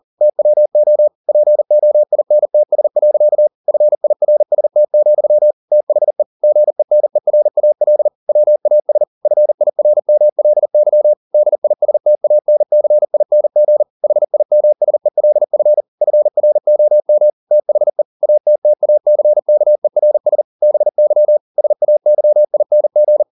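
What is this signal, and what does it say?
Morse code, heard as a keyed tone, about 40 wpm, 600 Hz. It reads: TWO POINTS. FIRST, THE GENERAL WAS FIRMLY DISTANCING HIMSELF FROM THE ATTACKERS BY SAYING
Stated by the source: Morse code at 40 wpm, practice text, CW med Gnister.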